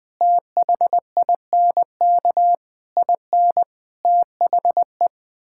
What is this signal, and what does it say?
Morse code, heard as a keyed tone, about 20 wpm, 700 Hz. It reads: THINK IN THE
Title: Sets of 8 Words - 20wpm